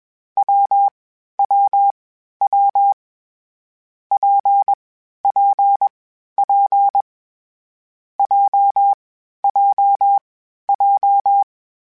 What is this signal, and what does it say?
Morse code, heard as a keyed tone, about 21 wpm, 800 Hz. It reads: W W W P P P J J J
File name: Variable Speed / Variable Speed-mp3s / WPJ WPJ1 sp @20